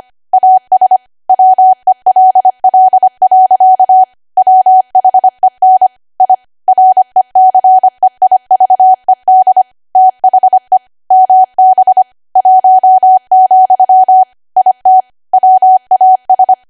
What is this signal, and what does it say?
Morse code, heard as a keyed tone, about 25 wpm, 750 Hz. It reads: AS WELL. WHEN I RECEIVED THE MB 1, IT WAS